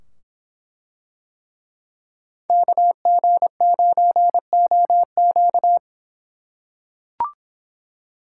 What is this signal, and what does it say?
Morse code, heard as a keyed tone, about 24 wpm, 700 Hz. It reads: KG9OQ